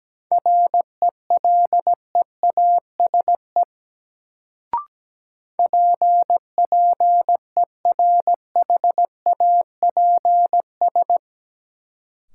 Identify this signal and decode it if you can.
Morse code, heard as a keyed tone, about 17 wpm, 700 Hz. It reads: RELEASE PPERHAPS
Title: Sentences from Top 1100 Words - New Word Drill - 17wpm